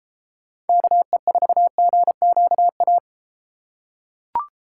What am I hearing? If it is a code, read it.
Morse code, heard as a keyed tone, about 31 wpm, 700 Hz. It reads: KE4GQA